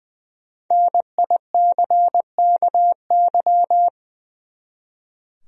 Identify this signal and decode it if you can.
Morse code, heard as a keyed tone, about 20 wpm, 700 Hz. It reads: NICKY
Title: USA Common Names - 20wpm